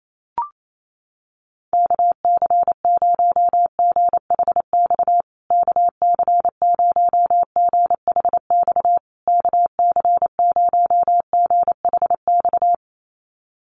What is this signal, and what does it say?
Morse code, heard as a keyed tone, about 28 wpm, 700 Hz. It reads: KC0GHX KC0GHX KC0GHX
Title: USA Call Signs Repeated 3x 28wpm